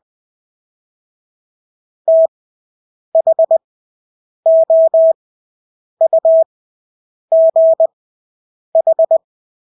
Morse code, heard as a keyed tone, about 20 wpm, spaced as for 8 wpm, 650 Hz. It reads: THOUGH
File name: Engelske ord 20/8 wpm